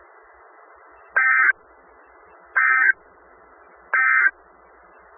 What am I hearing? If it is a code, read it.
Emergency Alert System (SAME) tones, heard as harsh buzzing bursts. It is heard from 1.1 to 4.3 s.
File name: Skywarn/2013/2013-03-18-Newton County WA4ASI Repeater